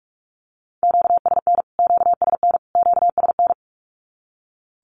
Morse code, heard as a keyed tone, about 45 wpm, 700 Hz. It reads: QSN QSN QSN
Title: Q-Codes Repeated 3x 45wpm